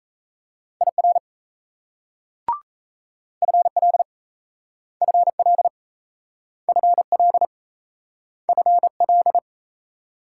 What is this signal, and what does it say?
Morse code, heard as a keyed tone, about 38 wpm, 700 Hz. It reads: IR FL FL FL FL